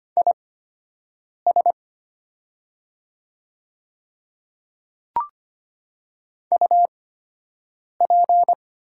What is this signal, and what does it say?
Morse code, heard as a keyed tone, about 25 wpm, spaced as for 7 wpm, 700 Hz. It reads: IS UP
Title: 8x Character Spacing - Top 100 Words - 25wpm